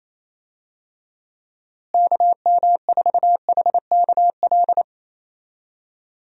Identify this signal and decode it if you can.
Morse code, heard as a keyed tone, about 28 wpm, 700 Hz. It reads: KM4HKL
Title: USA Call Signs - 6 Characters - Repeated 2x - 28wpm